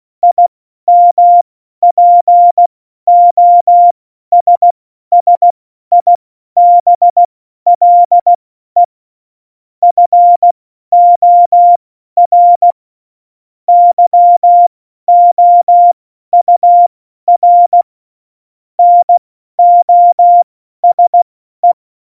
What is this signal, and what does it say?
Morse code, heard as a keyed tone, about 16 wpm, spaced as for 12 wpm, 700 Hz. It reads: IMPOSSIBLE FOR YOUR NOSE
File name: Morsekurs / Witze